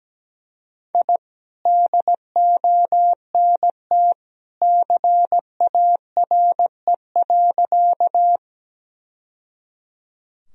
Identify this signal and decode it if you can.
Morse code, heard as a keyed tone, about 17 wpm, 700 Hz. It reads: I DONT CARE.
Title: Sentences from Top 1000 Words - Review Part 1 - 17wpm